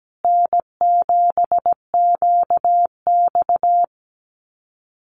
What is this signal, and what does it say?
Morse code, heard as a keyed tone, about 17 wpm, 700 Hz. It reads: N7QX
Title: USA Call Signs Repeated 3x 17wpm